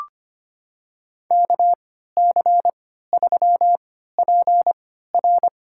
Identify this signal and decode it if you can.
Morse code, heard as a keyed tone, about 25 wpm, spaced as for 14 wpm, 700 Hz. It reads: KC3PR
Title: USA Call Signs - 3x Character Spacing - 25wpm